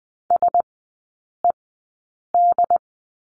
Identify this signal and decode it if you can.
Morse code, heard as a keyed tone, about 19 wpm, spaced as for 8 wpm, 700 Hz. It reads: SED
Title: 074 Lesson 13e - Letter L - Three Characters 20wpm